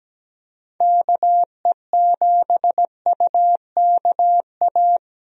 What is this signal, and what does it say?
Morse code, heard as a keyed tone, about 17 wpm, 700 Hz. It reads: KE7UKA
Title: USA Call Signs - 6 Characters - Repeated 2x - 17wpm